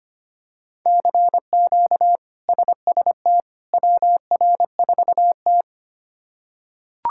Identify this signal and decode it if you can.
Morse code, heard as a keyed tone, about 25 wpm, 700 Hz. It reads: CQ SST WR4T
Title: K1USN SST QSO Practice - 25wpm